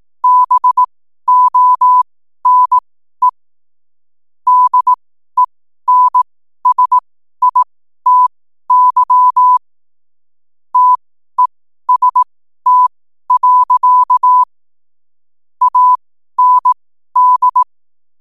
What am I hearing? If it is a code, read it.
Morse code, heard as a keyed tone, about 18 wpm, spaced as for 12 wpm, 1000 Hz. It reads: BONE DENSITY TEST. AND
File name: n7net / 12wpm